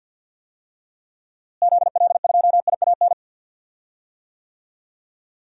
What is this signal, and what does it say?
Morse code, heard as a keyed tone, about 52 wpm, 700 Hz. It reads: ZL2IAN
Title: DXCC-100 Call Signs - 6 Characters - Repeated 3x - 50wpm